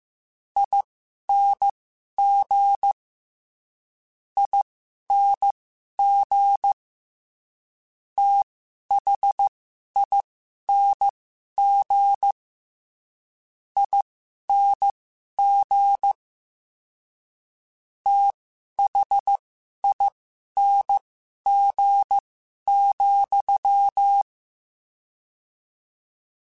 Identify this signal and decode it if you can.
Morse code, heard as a keyed tone, about 15 wpm, spaced as for 11 wpm, 800 Hz. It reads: ING ING THING ING THING,
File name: Special INGs / Special INGs-mp3s / VerySlowToFaster Ing @10